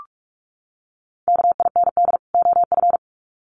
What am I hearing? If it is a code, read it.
Morse code, heard as a keyed tone, about 44 wpm, 700 Hz. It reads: KIND OF
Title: Sets of 2 Words 45wpm